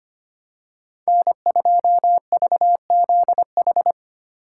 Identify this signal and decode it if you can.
Morse code, heard as a keyed tone, about 25 wpm, 700 Hz. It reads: N2VZH
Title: USA Call Signs - 4x Character Spacing - 25wpm